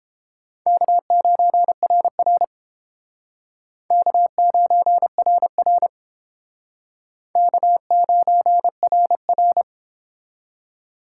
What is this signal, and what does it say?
Morse code, heard as a keyed tone, about 29 wpm, 700 Hz. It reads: K9RR K9RR K9RR